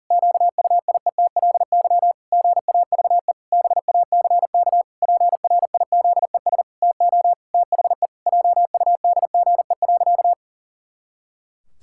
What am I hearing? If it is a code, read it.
Morse code, heard as a keyed tone, about 40 wpm, 700 Hz. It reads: QUIETLY GAVE BACK PRIZES TO THE JUDGE.
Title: Pangram Challenge v2 - 40wpm